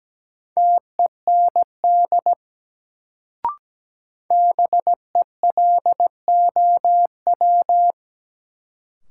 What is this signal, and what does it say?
Morse code, heard as a keyed tone, about 17 wpm, 700 Hz. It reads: TEND BELOW
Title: Sentences from Top 1000 Words - Review Part 5 - New Word Drill - 17wpm